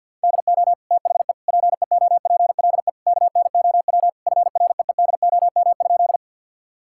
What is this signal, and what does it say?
Morse code, heard as a keyed tone, about 50 wpm, 700 Hz. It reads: DO THE PEOPLE KNOW FREEDOM?
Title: Sentences from Top 1000 Words - Review Part 5 - 50wpm